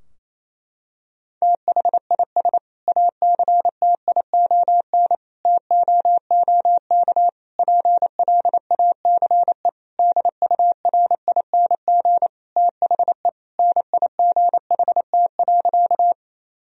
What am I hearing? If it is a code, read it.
Morse code, heard as a keyed tone, about 28 wpm, 700 Hz. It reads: THIS ACTION TOOK PLACE DURING THE NIGHT.
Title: Sentences from Top 1000 Words - Review Part 4 - 28wpm